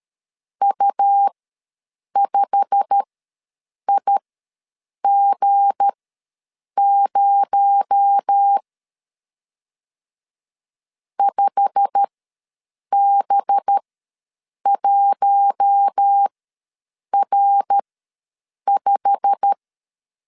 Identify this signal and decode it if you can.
Morse code, heard as a keyed tone, about 13 wpm, spaced as for 7 wpm, 800 Hz. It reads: U5IG0 5B1R5
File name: LEZIONI / 8 YJ45 / 8 YJ45 ws 12wpm 3146